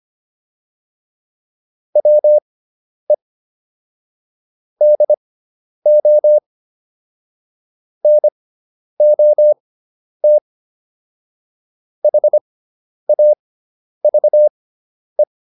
Morse code, heard as a keyed tone, about 25 wpm, spaced as for 10 wpm, 600 Hz. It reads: WE DO NOT HAVE